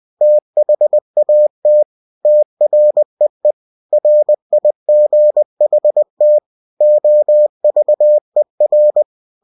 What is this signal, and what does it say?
Morse code, heard as a keyed tone, about 20 wpm, 600 Hz. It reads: THAT TREE RIGHT OVER
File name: cw-books / winnie-the-pooh / winnie-20wpm-page-0069